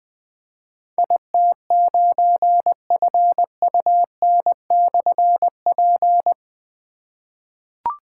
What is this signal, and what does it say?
Morse code, heard as a keyed tone, about 20 wpm, 700 Hz. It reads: IT9FUN/P
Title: Difficult Call Signs 20wpm